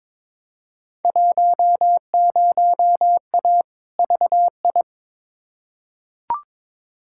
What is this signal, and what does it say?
Morse code, heard as a keyed tone, about 22 wpm, 700 Hz. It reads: 10A VI
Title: Field Day Exchanges Encore - 22wpm